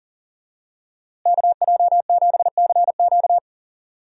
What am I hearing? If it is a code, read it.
Morse code, heard as a keyed tone, about 40 wpm, 700 Hz. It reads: KJ7CQ
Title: USA Call Signs - 5 Characters - Repeated 3x - 40wpm